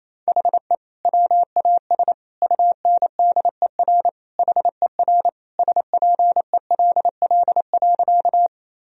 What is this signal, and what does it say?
Morse code, heard as a keyed tone, about 28 wpm, 700 Hz. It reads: HE WAS UNDER HER SPELL.
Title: Sentences from Top 1000 Words - Review Part 1 - 28wpm